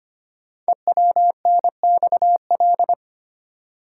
Morse code, heard as a keyed tone, about 25 wpm, 690 Hz. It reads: EWNXL
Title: Code Groups - 5 Letters - 25wpm